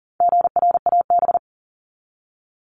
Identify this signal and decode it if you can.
Morse code, heard as a keyed tone, about 40 wpm, 700 Hz. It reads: GRAB
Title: Top 1751-2000 Words 40wpm